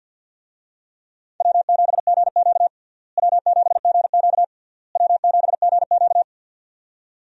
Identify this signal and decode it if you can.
Morse code, heard as a keyed tone, about 49 wpm, 700 Hz. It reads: W7GQ W7GQ W7GQ
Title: USA Call Signs - 4 Characters - Repeated 3x - 50wpm